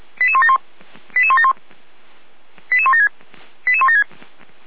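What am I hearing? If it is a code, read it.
Five-tone selective call (ZVEI), heard as quick runs of five tones, tones of about 70 ms. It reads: 88171 88171 88176 88176
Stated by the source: Five-tone ZVEI selective call 89457, received via SDR.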